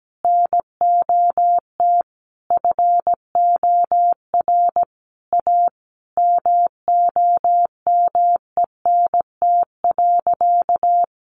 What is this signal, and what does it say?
Morse code, heard as a keyed tone, about 17 wpm, 700 Hz. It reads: NOT FOR A MOMENT.
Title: Sentences from Top 1000 Words 17wpm